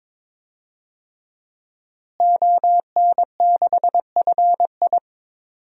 Morse code, heard as a keyed tone, about 22 wpm, 700 Hz. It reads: ON6FI
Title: DXCC-100 Call Signs - 5 Characters - Repeated 3x - 22wpm